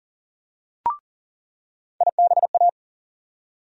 Morse code, heard as a keyed tone, about 40 wpm, 700 Hz. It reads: IBA
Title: USA Call Sign Postfixes - 3 Letters 40wpm